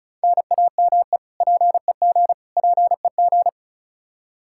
Morse code, heard as a keyed tone, about 36 wpm, 700 Hz. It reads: NAME PEG PEG
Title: Practice QSO - Exchange 1 - Beginning - 35wpm